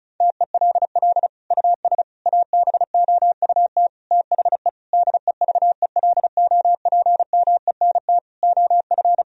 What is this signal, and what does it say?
Morse code, heard as a keyed tone, about 36 wpm, 700 Hz. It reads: TELL US ABOUT THE DEVELOPMENT OF
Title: Sentences from Top 1000 Words - Review Part 4 - 35wpm